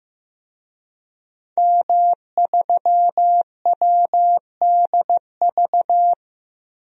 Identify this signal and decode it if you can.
Morse code, heard as a keyed tone, about 15 wpm, 700 Hz. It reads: M3WDV
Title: DXCC-100 Call Signs - 5 Characters - 15wpm